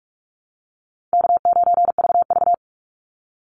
Morse code, heard as a keyed tone, about 44 wpm, 700 Hz. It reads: K9VV
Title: Easy Call Signs Repeated 3x - 45wpm